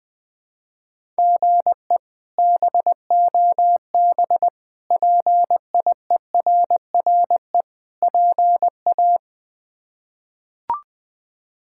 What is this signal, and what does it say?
Morse code, heard as a keyed tone, about 20 wpm, 700 Hz. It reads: GE BOB PIERRE PA
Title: K1USN SST QSO Practice - 20wpm